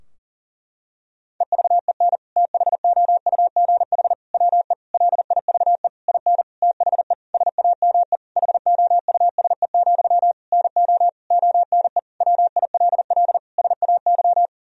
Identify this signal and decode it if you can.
Morse code, heard as a keyed tone, about 40 wpm, 700 Hz. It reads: EVEN THOUGH WE LIVE IN THE SAME HOUSE, NO ONE WILL SAY
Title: Sentences from Top 1000 Words - Review Part 2 - 40wpm